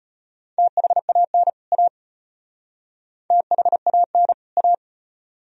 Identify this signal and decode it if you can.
Morse code, heard as a keyed tone, about 32 wpm, 700 Hz. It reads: THAN A THAN A